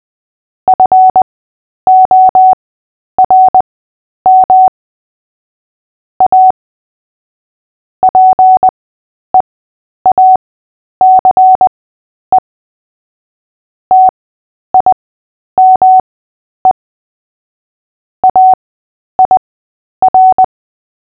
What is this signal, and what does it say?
Morse code, heard as a keyed tone, about 20 wpm, spaced as for 10 wpm, 750 Hz. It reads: FORM A PEACE TIME AIR
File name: Short-Story-7-10wpm